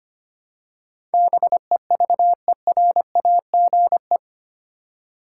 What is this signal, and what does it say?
Morse code, heard as a keyed tone, about 25 wpm, 700 Hz. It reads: BEVERAGE
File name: Eight Letter Words - 25wpm